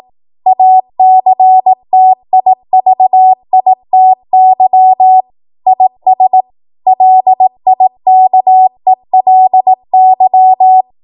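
Morse code, heard as a keyed tone, about 18 wpm, 750 Hz. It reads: ACTIVITY IS LIKELY
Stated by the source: Code practice at 18 WPM, a transition run bridging 15 to 20 WPM.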